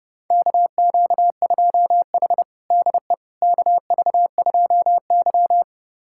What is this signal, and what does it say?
Morse code, heard as a keyed tone, about 32 wpm, 700 Hz. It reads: KQ2H DE KV2Y
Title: USA Callsign DE Callsign - 4 Characters - 30wpm